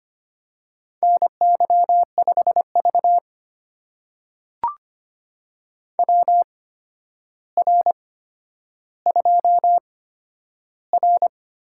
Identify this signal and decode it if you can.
Morse code, heard as a keyed tone, about 25 wpm, spaced as for 11 wpm, 700 Hz. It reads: NY5V WR2R
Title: USA Call Signs - 4 Characters - 8x Character Spacing - 25wpm